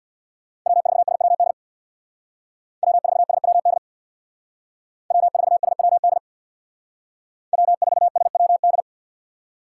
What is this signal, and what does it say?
Morse code, heard as a keyed tone, about 62 wpm, 700 Hz. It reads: W4SWD W4SWD W4SWD W4SWD
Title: Speed Racing - USA Call Signs 50wpm